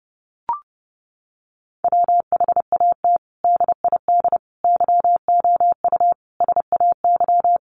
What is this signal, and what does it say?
Morse code, heard as a keyed tone, about 28 wpm, 700 Hz. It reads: WHAT DID YOU SAY
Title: Sentences from Top 200 Words 30wpm